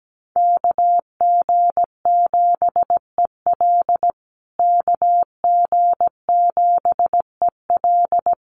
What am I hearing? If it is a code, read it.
Morse code, heard as a keyed tone, about 17 wpm, 700 Hz. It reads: KG7EL KG7EL
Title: USA Call Signs Repeated 3x 17wpm